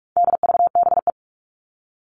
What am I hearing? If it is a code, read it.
Morse code, heard as a keyed tone, about 44 wpm, 700 Hz. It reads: DVBE
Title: All Cut Numbers - 4-Digit 45wpm